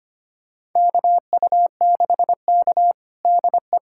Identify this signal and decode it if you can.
Morse code, heard as a keyed tone, about 25 wpm, 700 Hz. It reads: KU6K DE
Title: USA Callsign DE Callsign - 4 Characters - 25wpm